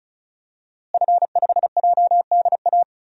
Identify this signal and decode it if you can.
Morse code, heard as a keyed tone, about 34 wpm, 700 Hz. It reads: F5JDA
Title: DXCC-100 Call Signs - 5 Characters - Repeated 3x - 35wpm